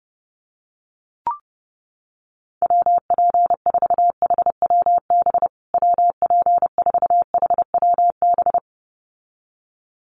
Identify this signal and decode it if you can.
Morse code, heard as a keyed tone, about 30 wpm, 700 Hz. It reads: WP4HWB WP4HWB
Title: USA Call Signs Repeated 2x 30wpm